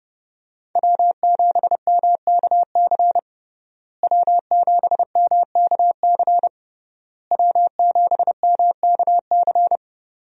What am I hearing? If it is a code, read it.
Morse code, heard as a keyed tone, about 32 wpm, 700 Hz. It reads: W7MKC W7MKC W7MKC
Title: USA Call Signs - 5 Characters - Repeated 3x - 30wpm